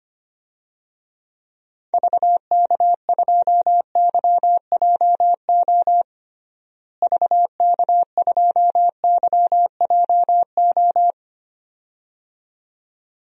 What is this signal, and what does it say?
Morse code, heard as a keyed tone, about 25 wpm, 700 Hz. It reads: VK2YJO VK2YJO
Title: DXCC-100 Call Signs - 6 Characters - Repeated 2x - 25wpm